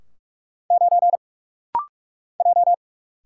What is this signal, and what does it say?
Morse code, heard as a keyed tone, about 44 wpm, 700 Hz. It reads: 9 J
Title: Single Letter-Number - Rapid-Fire 45wpm